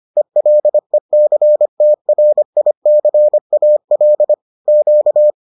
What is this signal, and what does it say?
Morse code, heard as a keyed tone, about 25 wpm, 600 Hz. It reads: ELECTRICAL Q